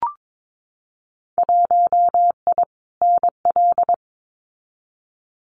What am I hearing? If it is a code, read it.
Morse code, heard as a keyed tone, about 22 wpm, 700 Hz. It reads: 1I NL
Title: Winter Field Day Exchanges 22wpm